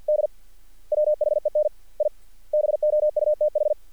Morse code, heard as a keyed tone, about 50 wpm, 600 Hz. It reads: D WHEN I BOUTH
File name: archived / CRSnet